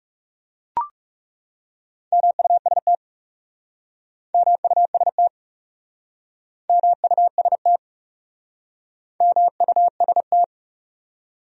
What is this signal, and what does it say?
Morse code, heard as a keyed tone, about 40 wpm, 700 Hz. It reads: MUST MUST MUST MUST